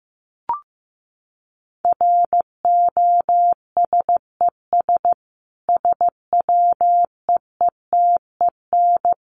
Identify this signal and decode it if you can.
Morse code, heard as a keyed tone, about 15 wpm, 700 Hz. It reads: ROSES SWEETEN
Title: A Sound Of Thunder 15wpm